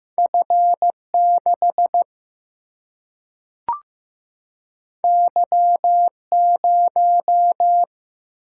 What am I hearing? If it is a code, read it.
Morse code, heard as a keyed tone, about 15 wpm, 700 Hz. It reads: F6 Y0
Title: Single-Letter Single-Number 15wpm